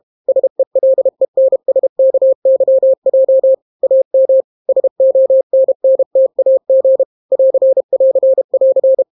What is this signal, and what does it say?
Morse code, heard as a keyed tone, about 32 wpm, 530 Hz. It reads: SELENSKYJ AM SONNTAG <AR><AR><AR>